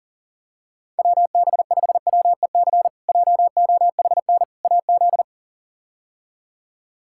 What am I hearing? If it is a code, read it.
Morse code, heard as a keyed tone, about 40 wpm, 700 Hz. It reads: WB5WEC JOHN AZ
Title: K1USN SST QSO Practice - 40wpm